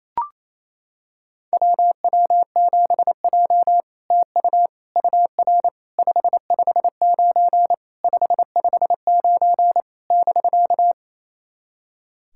Morse code, heard as a keyed tone, about 28 wpm, 700 Hz. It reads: WW7J TU UR 559 559 <BK>
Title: SOTA QSO Practice 28wpm